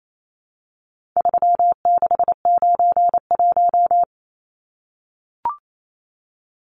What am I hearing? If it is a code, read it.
Morse code, heard as a keyed tone, about 28 wpm, 700 Hz. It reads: 3691
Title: Numbers - 4-Digit 28wpm